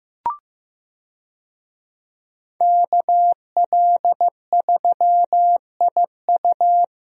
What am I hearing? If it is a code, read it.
Morse code, heard as a keyed tone, about 15 wpm, 700 Hz. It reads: KL3IU
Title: USA Call Signs - 5 Characters - Repeated 3x - 15wpm